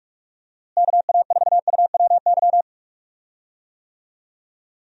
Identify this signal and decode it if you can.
Morse code, heard as a keyed tone, about 42 wpm, 700 Hz. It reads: KA4UWY
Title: USA Call Signs - 6 Characters - Repeated 3x - 45wpm